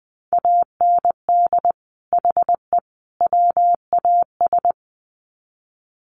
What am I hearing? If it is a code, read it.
Morse code, heard as a keyed tone, about 20 wpm, 700 Hz. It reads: AND HE WAS
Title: Sets of 3 Words 20wpm